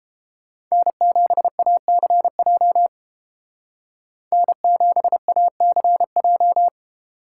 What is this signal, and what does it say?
Morse code, heard as a keyed tone, about 29 wpm, 700 Hz. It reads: N7ACJ N7ACJ